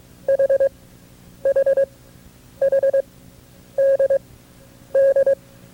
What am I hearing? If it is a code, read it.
Morse code, heard as a keyed tone, about 22 wpm, spaced as for 9 wpm, 550 Hz. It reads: HHHDD
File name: FreeFiles / CW-practice / KB6MT-tape2-side2